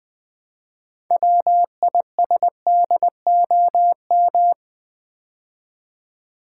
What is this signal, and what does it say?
Morse code, heard as a keyed tone, about 20 wpm, 700 Hz. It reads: WISDOM
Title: Build A Word - 2 Letters At A Time - 6-Letter Words - 20wpm